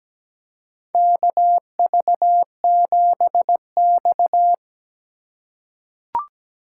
Morse code, heard as a keyed tone, about 17 wpm, 700 Hz. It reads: KV7X
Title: USA Call Signs - 4 Characters - 4x Character Spacing - 17wpm